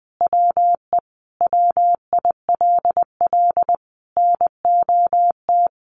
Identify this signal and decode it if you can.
Morse code, heard as a keyed tone, about 20 wpm, 700 Hz. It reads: WE WILL NOT
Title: Sentences from Top 1100 Words 20wpm